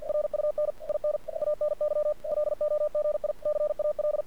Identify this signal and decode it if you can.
Morse code, heard as a keyed tone, about 48 wpm, 600 Hz. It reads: RUN IN ANY LOGICAL
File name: archived / CRSnet